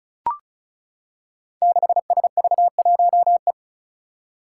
Morse code, heard as a keyed tone, about 36 wpm, 700 Hz. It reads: 6SV1E
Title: Five Challenging Letter-Number Combinations 35wpm